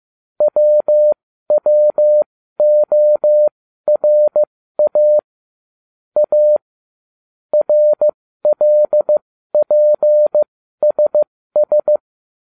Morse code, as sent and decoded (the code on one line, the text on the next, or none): .-- .-- --- .-. .- / .- / .-. .-.. .--. ... ...
WWORA A RLPSS